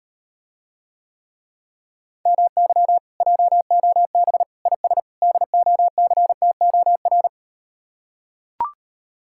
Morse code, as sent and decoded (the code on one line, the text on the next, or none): -- -.-- / .--- --- -... / .. ... / -.. --- -.-. - --- .-.
MY JOB IS DOCTOR